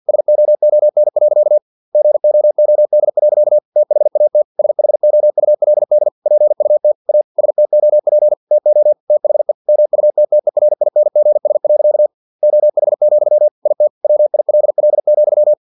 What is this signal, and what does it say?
Morse code, heard as a keyed tone, about 50 wpm, 600 Hz. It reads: SOON. GOOD. THAT SHOULD PUT A STOP TO THE MUTTERINGS. OH, IT WILL,
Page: Never 49wpm